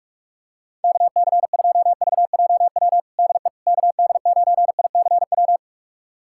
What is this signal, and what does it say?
Morse code, heard as a keyed tone, about 46 wpm, 700 Hz. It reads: KC2VJW DE KD9ICW
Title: USA Callsign DE Callsign 45wpm